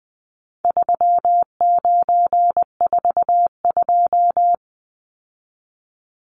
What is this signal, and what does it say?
Morse code, heard as a keyed tone, about 20 wpm, 700 Hz. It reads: 3942
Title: Numbers - 4-Digit 20wpm